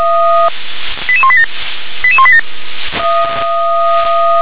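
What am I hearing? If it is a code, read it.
Five-tone selective call (ZVEI), heard as quick runs of five tones, tones of about 70 ms. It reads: 88187 88187
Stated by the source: ZVEI five-tone selective call received on SDR.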